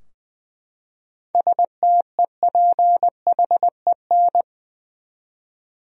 Morse code, heard as a keyed tone, about 20 wpm, 700 Hz. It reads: STEPHEN
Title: USA Common Names - 20wpm